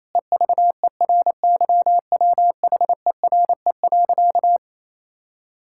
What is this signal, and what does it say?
Morse code, heard as a keyed tone, about 28 wpm, 700 Hz. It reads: EVERYWHERE.
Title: Sentences from Top 1600 Words 28wpm